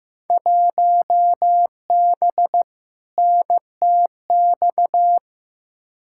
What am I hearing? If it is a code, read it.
Morse code, heard as a keyed tone, about 15 wpm, 700 Hz. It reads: 1B NTX
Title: Field Day Exchanges - 15wpm